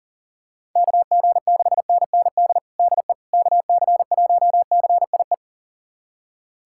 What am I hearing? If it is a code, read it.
Morse code, heard as a keyed tone, about 37 wpm, 700 Hz. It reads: KG6NND DE KC1CIE
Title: USA Callsign DE Callsign - 6 Characters - 40wpm